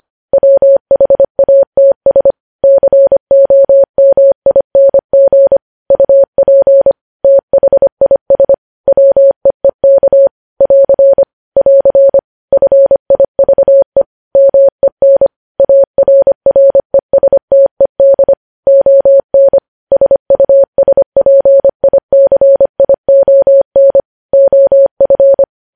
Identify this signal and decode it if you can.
Morse code, heard as a keyed tone, about 25 wpm, 550 Hz. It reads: WHATS COMING UP THIS WEEK <AR> <AR> FIVE MEN ARRESTED ON SUSPICION OF
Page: News Headlines in Morse Code at 25 WPM